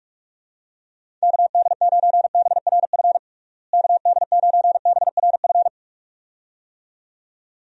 Morse code, heard as a keyed tone, about 46 wpm, 700 Hz. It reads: KD9BRF KD9BRF
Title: USA Call Signs - 6 Characters - Repeated 2x - 45wpm